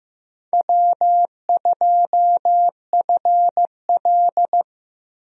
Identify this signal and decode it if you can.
Morse code, heard as a keyed tone, about 15 wpm, 700 Hz. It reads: W2FL
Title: USA Call Signs - 4 Characters - Repeated 2x - 15wpm